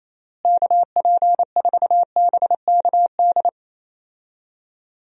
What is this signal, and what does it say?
Morse code, heard as a keyed tone, about 28 wpm, 700 Hz. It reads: KP4BKD
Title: DXCC-100 Call Signs - 6 Characters - Repeated 3x - 28wpm